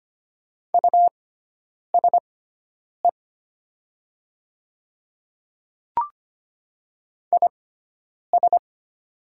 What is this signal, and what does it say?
Morse code, heard as a keyed tone, about 25 wpm, spaced as for 9 wpm, 700 Hz. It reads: USE IS
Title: 6x Character Spacing - Top 100 Words - 25wpm